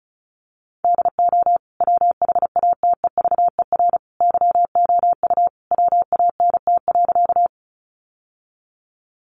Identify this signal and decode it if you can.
Morse code, heard as a keyed tone, about 34 wpm, 700 Hz. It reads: DO WHATEVER YOU WANT.
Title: Sentences from Top 1100 Words 35wpm